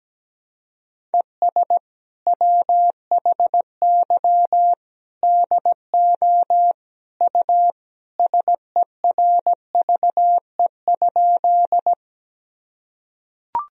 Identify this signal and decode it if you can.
Morse code, heard as a keyed tone, about 17 wpm, 700 Hz. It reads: ES WHY DO U SERVE?
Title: Sentences from Top 400 Words and Abbreviations 17wpm